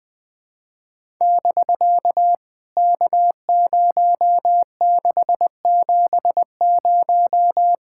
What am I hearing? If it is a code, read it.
Morse code, heard as a keyed tone, about 19 wpm, 700 Hz. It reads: <BK> K0670
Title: POTA QSO Practice - 20wpm